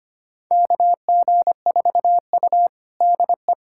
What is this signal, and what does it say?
Morse code, heard as a keyed tone, about 25 wpm, 700 Hz. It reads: KG4U DE
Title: USA Callsign DE Callsign - 4 Characters - 25wpm